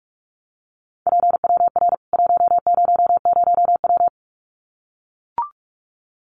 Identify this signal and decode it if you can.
Morse code, heard as a keyed tone, about 44 wpm, 700 Hz. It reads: PWR 100W
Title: Common QSO Phrases 45wpm